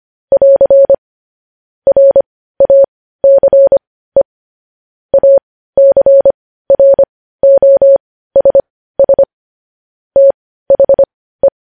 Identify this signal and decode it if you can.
Morse code, heard as a keyed tone, about 25 wpm, spaced as for 15 wpm, 550 Hz. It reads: <AR> RACE ACROSS THE
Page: news-headlines-in-morse-code-at-15-wpm